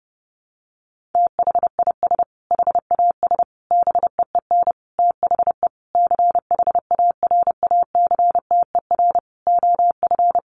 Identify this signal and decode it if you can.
Morse code, heard as a keyed tone, about 30 wpm, 700 Hz. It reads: THIS HAS BEEN THE CHARACTER OF
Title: Sentences from Top 1400 Words 30wpm